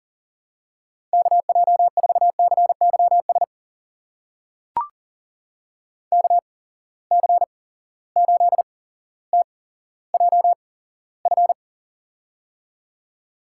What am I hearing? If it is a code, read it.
Morse code, heard as a keyed tone, about 40 wpm, spaced as for 17 wpm, 700 Hz. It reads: KJ4CYS KC8TJF
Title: USA Call Signs - 6 Characters - 8x Character Spacing - 40wpm